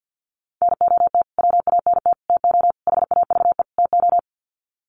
Morse code, heard as a keyed tone, about 52 wpm, 700 Hz. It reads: NOT WANT TO HAVE TO
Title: Sets of 5 Words Encore2 50wpm